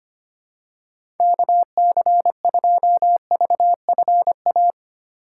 Morse code, heard as a keyed tone, about 25 wpm, 700 Hz. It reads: KC2VFA
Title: USA Call Signs - 6 Characters - 3x Character Spacing - 25wpm